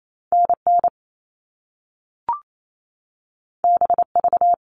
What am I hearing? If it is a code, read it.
Morse code, heard as a keyed tone, about 28 wpm, 700 Hz. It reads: NN BV
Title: All Cut Numbers - 2-Digit 28wpm